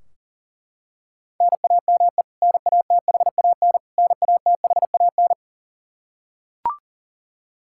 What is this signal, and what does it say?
Morse code, heard as a keyed tone, about 40 wpm, 700 Hz. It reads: NAME NATHAN NATHAN
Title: Practice QSO - Exchange 1 - Beginning - 40wpm